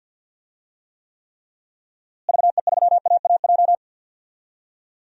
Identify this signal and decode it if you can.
Morse code, heard as a keyed tone, about 52 wpm, 700 Hz. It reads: VE3AAJ
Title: DXCC-100 Call Signs - 6 Characters - Repeated 3x - 50wpm